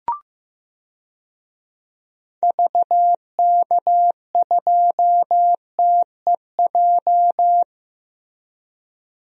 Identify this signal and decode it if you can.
Morse code, heard as a keyed tone, about 15 wpm, 700 Hz. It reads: VK2TEJ
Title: DXCC-100 Call Signs - 6 Characters - Repeated 2x - 15wpm